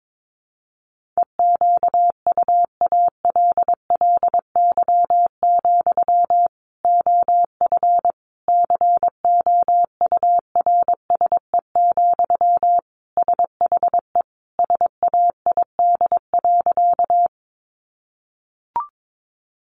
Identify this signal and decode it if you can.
Morse code, heard as a keyed tone, about 22 wpm, 700 Hz. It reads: EQUALLY, OF COURSE, SHE SAID.
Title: Sentences from Top 1400 Words 22wpm